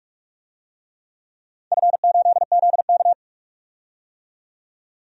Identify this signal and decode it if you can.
Morse code, heard as a keyed tone, about 46 wpm, 700 Hz. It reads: F8ZK
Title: DXCC-100 Call Signs - 4 Characters - 45wpm